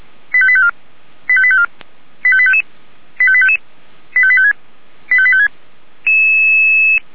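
Five-tone selective call (ZVEI), heard as quick runs of five tones, tones of about 80 ms. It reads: 85854 85854 85855 85855 85856 85856
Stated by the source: ZVEI five-tone selective call received on an SDR.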